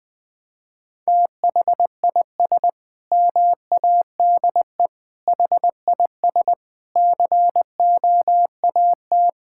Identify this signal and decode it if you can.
Morse code, heard as a keyed tone, about 20 wpm, 700 Hz. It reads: THIS MADE HIS COAT
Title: Sentences from Top 1600 Words 20wpm